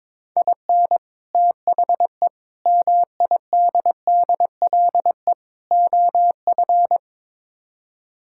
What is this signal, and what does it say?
Morse code, heard as a keyed tone, about 22 wpm, 700 Hz. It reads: IN THE MIDDLE OF
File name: practice / 22wpm / Build A Phrase - 5-Word Phrases - 22wpm